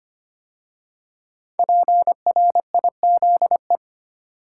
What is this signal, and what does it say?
Morse code, heard as a keyed tone, about 25 wpm, 700 Hz. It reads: PRIZE
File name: 5 Letter Words - First 3 Letters Given - 25wpm